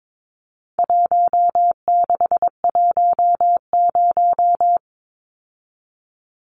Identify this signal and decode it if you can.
Morse code, heard as a keyed tone, about 22 wpm, 700 Hz. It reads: 1610
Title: Numbers - 4-Digit 22wpm